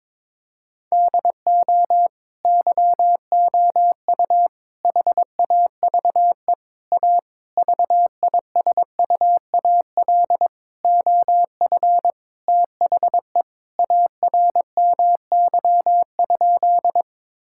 Morse code, heard as a keyed tone, about 22 wpm, 700 Hz. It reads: DO YOU HAVE A VISUAL OF THE ARMY?
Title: Sentences from Top 1000 Words - Review Part 6 - 22wpm